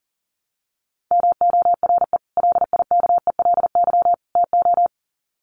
Morse code, heard as a keyed tone, about 40 wpm, 700 Hz. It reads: MORE LIKELY TO
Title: Sets of 3 Words 40wpm